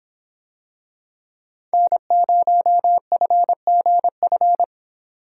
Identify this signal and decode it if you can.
Morse code, heard as a keyed tone, about 26 wpm, 700 Hz. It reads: N0FGF